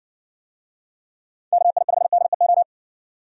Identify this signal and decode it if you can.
Morse code, heard as a keyed tone, about 59 wpm, 700 Hz. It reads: KI5DEO